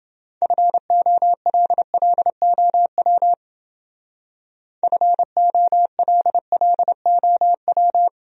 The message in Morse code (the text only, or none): FOLLOW FOLLOW